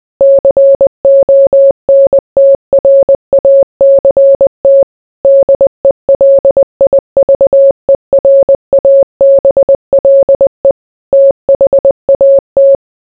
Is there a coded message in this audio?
Yes: Morse code from 0.2 to 12.8 s, heard as a keyed tone, about 20 wpm, 550 Hz. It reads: CONTRACT DELIVERABLE THAT